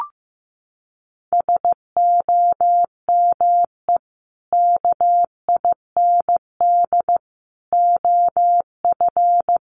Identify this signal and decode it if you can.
Morse code, heard as a keyed tone, about 15 wpm, 700 Hz. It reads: SOME KIND OF
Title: Sets of 3 Words 15wpm